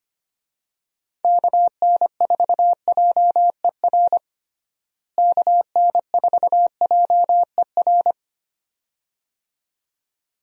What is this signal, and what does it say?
Morse code, heard as a keyed tone, about 25 wpm, 700 Hz. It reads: KN4JER KN4JER
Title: USA Call Signs - 6 Characters - Repeated 2x - 25wpm